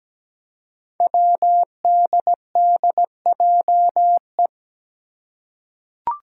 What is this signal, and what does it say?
Morse code, heard as a keyed tone, about 17 wpm, 700 Hz. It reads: WDDJE
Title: Five Challenging Letter-Number Combinations 17wpm